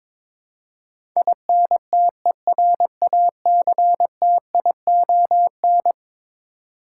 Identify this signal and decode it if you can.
Morse code, heard as a keyed tone, about 22 wpm, 700 Hz. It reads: INTERACTION
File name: practice / 22wpm / Eleven Letter Words - 22wpm